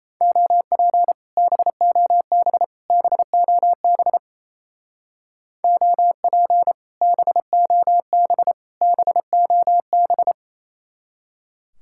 Morse code, heard as a keyed tone, about 34 wpm, 700 Hz. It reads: OP BOB BOB OP BOB BOB
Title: Speed-Racing Common QSO Phrases - 28wpm